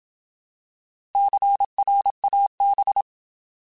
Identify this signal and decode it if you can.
Morse code, heard as a keyed tone, about 27 wpm, 800 Hz. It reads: CRAB
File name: Alphabetical / Alphabetical mp3s / Cs @25